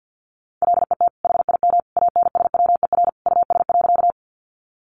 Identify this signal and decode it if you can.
Morse code, heard as a keyed tone, about 50 wpm, 700 Hz. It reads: LET HIM ANSWER US.
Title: Sentences from Top 300 Words 50wpm